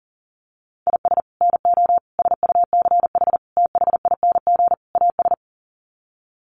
Morse code, heard as a keyed tone, about 40 wpm, 700 Hz. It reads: IS NO SUCH THING AS
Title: Sets of 5 Words Encore 40wpm